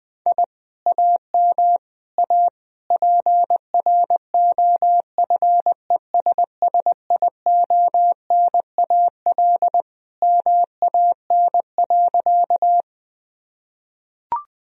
Morse code, heard as a keyed tone, about 20 wpm, 700 Hz. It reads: I AM A PROFESSIONAL MAN.
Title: Sentences from Top 1000 Words - Review Part 4 - 20wpm